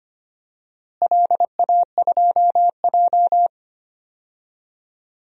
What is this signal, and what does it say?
Morse code, heard as a keyed tone, about 25 wpm, 700 Hz. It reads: LA2J
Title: DXCC-100 Call Signs - 4 Characters - Repeated 2x - 25wpm